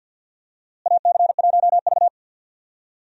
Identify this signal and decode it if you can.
Morse code, heard as a keyed tone, about 47 wpm, 700 Hz. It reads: AC1V
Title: USA Call Signs - 4 Characters - Repeated 3x - 50wpm